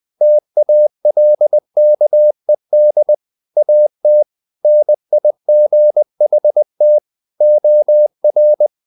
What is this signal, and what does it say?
Morse code, heard as a keyed tone, about 20 wpm, 600 Hz. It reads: TALKED AT NIGHT OR
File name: the old man and the sea / the old man and the sea part 1 20wpm